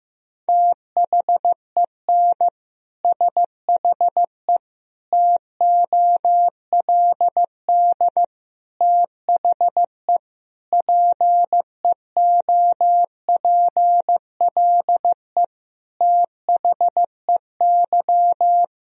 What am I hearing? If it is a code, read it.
Morse code, heard as a keyed tone, about 15 wpm, 700 Hz. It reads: THEN SHE TOLD THE PEOPLE THEY
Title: Sentences from Top 1000 Words - Review Part 2 - 15wpm